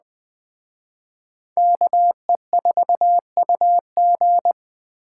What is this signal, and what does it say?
Morse code, heard as a keyed tone, about 20 wpm, 700 Hz. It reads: KE4UG
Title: Speed Racing - USA 2x2 Call Signs 17wpm